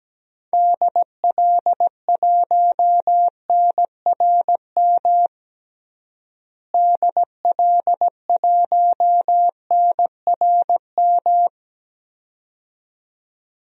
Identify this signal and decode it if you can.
Morse code, heard as a keyed tone, about 17 wpm, 700 Hz. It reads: DL1NRM DL1NRM
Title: DXCC-100 Call Signs - 6 Characters - Repeated 2x - 17wpm